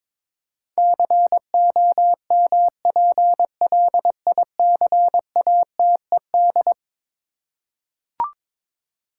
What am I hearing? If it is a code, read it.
Morse code, heard as a keyed tone, about 22 wpm, 700 Hz. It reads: COMPLICATED
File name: practice / 22wpm / Eleven Letter Words - 22wpm